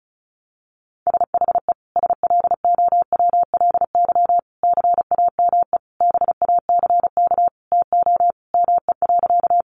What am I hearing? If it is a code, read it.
Morse code, heard as a keyed tone, about 34 wpm, 700 Hz. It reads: SHE SLOWLY CAME BACK TO ME.
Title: Sentences from Top 1100 Words 35wpm